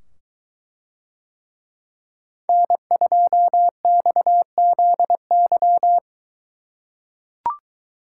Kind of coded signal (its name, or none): Morse code